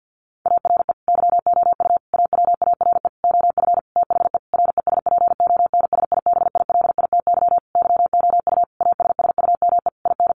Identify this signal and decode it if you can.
Morse code, heard as a keyed tone, about 48 wpm, 700 Hz. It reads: ARE YOU AWARE OF THE RESPONSIBILITY YOU ASSUME IN
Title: Sentences from Top 1500 Words 50wpm